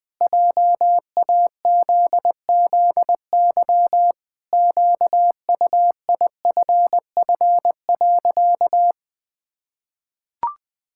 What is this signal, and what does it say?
Morse code, heard as a keyed tone, about 20 wpm, 700 Hz. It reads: JAZZY QUIFF.
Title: Pangram Challenge v2 - 20wpm